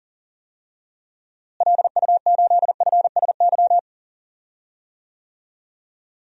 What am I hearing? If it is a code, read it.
Morse code, heard as a keyed tone, about 40 wpm, 700 Hz. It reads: LU8FSY
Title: DXCC-100 Call Signs - 6 Characters - Repeated 2x - 40wpm